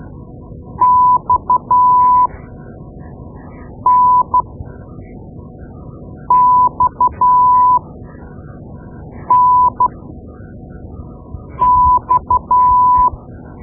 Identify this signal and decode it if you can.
Morse code, heard as a keyed tone, about 12 wpm, spaced as for 4 wpm, 1000 Hz. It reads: XNXNX
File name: Morse Lessons / Morse Lessons / Tape-2B